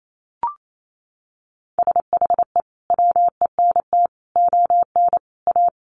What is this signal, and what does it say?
Morse code, heard as a keyed tone, about 28 wpm, 700 Hz. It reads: SHE WENT ON A
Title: Sentences from Top 1400 Words 28wpm